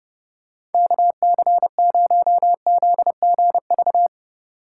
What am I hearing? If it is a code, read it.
Morse code, heard as a keyed tone, about 30 wpm, 700 Hz. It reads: KC0ZGV